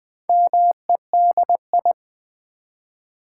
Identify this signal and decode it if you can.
Morse code, heard as a keyed tone, about 20 wpm, 700 Hz. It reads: MEDI
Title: Build A Word - 2 Letters At A Time - 8-Letter Words - 20wpm